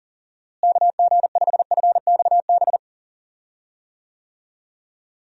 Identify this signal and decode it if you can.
Morse code, heard as a keyed tone, about 40 wpm, 700 Hz. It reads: KG5FXB
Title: USA Call Signs - 6 Characters - Repeated 3x - 40wpm